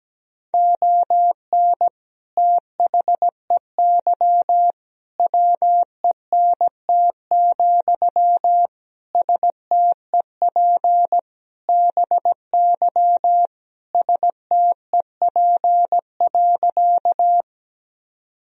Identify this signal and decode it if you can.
Morse code, heard as a keyed tone, about 17 wpm, 700 Hz. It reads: ON THEY WENT, STEP BY STEP.
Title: Sentences from Top 1000 Words - Review Part 2 - 17wpm